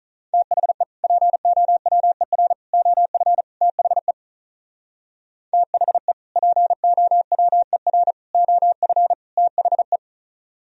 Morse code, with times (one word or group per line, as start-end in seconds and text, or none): 0.3-0.8 THE
1.0-2.5 POWER
2.7-3.4 OF
3.6-4.1 THE
5.5-6.1 THE
6.4-8.1 POWER
8.3-9.1 OF
9.4-10.0 THE